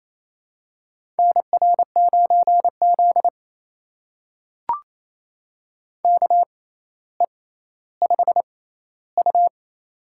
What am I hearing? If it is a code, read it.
Morse code, heard as a keyed tone, about 28 wpm, spaced as for 14 wpm, 700 Hz. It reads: NR9Z KE5U
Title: USA Call Signs - 4 Characters - 6x Character Spacing - 28wpm